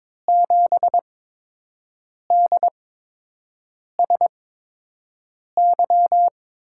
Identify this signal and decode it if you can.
Morse code, heard as a keyed tone, about 22 wpm, spaced as for 6 wpm, 700 Hz. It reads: 7DSY
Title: USA Call Signs - 6 Characters - 8x Character Spacing - 22wpm